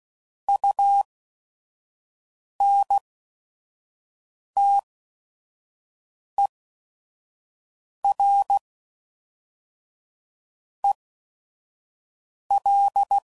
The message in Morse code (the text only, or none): U N T E R E L